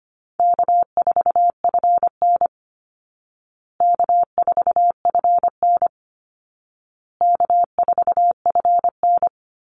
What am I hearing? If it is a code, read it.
Morse code, heard as a keyed tone, about 25 wpm, 700 Hz. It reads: K4FN K4FN K4FN
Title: Easy Call Signs Repeated 3x - 25wpm